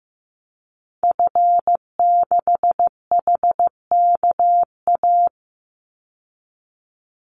Morse code, heard as a keyed tone, about 15 wpm, 700 Hz. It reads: F6HKA
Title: Call Signs Repeated 2x - 15wpm